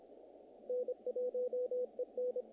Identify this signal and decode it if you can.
Morse code, heard as a keyed tone, about 25 wpm, 500 Hz, with radio noise. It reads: N1EN
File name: contests / naqp-cw-k5tr-op-k5pi- -1800